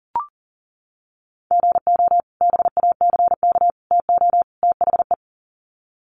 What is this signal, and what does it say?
Morse code, heard as a keyed tone, about 40 wpm, 700 Hz. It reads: GO BACK TO THE
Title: Sets of 4 Words 40wpm